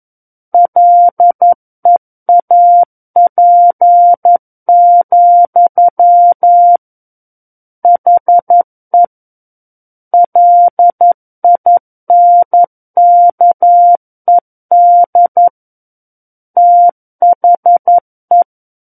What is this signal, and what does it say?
Morse code, heard as a keyed tone, about 11 wpm, 700 Hz. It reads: LEAP, HE LINKED THE
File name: 11 WPM